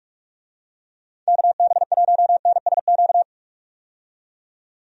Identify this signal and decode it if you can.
Morse code, heard as a keyed tone, about 41 wpm, 700 Hz. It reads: KB1NSQ
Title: USA Call Signs - 6 Characters - 45wpm